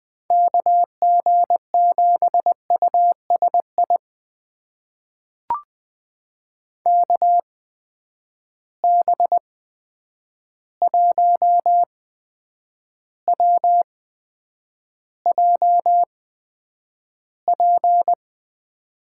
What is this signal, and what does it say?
Morse code, heard as a keyed tone, about 20 wpm, spaced as for 9 wpm, 700 Hz. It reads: KG7USI KB1WJP
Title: USA Call Signs - 6 Characters - 8x Character Spacing - 20wpm